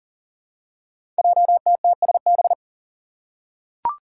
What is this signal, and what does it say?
Morse code, heard as a keyed tone, about 42 wpm, 700 Hz. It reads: JTTSB
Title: Five Challenging Letter-Number Combinations 40wpm